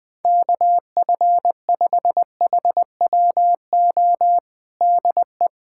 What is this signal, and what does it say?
Morse code, heard as a keyed tone, about 20 wpm, 700 Hz. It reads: KF5HWO DE
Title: USA Callsign DE Callsign - 6 Characters - 20wpm